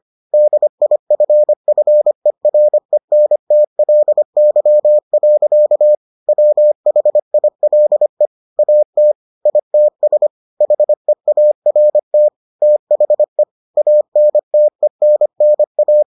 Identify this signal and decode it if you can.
Morse code, heard as a keyed tone, about 25 wpm, 600 Hz. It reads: DIFFERENTLY. WHILE AT ITS HEART THE ANTENNA